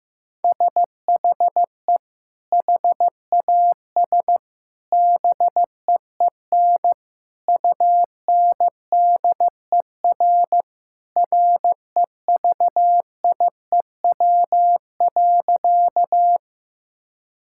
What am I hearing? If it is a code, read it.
Morse code, heard as a keyed tone, about 15 wpm, 700 Hz. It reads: SHE HAS BEEN UNDER REVIEW.
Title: Sentences from Top 1000 Words - Review Part 6 - 15wpm